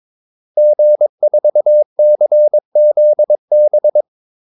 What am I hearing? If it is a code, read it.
Morse code, heard as a keyed tone, about 22 wpm, 600 Hz. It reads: G4CZB
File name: Callsigns-22wpm